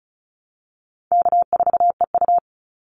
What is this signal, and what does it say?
Morse code, heard as a keyed tone, about 34 wpm, 700 Hz. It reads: K4EU
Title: Call Signs Repeated 3x - 35wpm